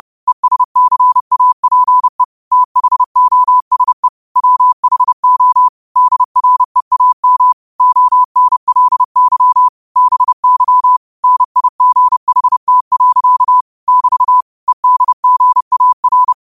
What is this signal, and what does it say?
Morse code, heard as a keyed tone, about 30 wpm, 1000 Hz. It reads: ESCAPE THOSE WHO DREAM ONLY BY NIGHT. <BT> EDGAR
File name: qotdfiles / 30wpm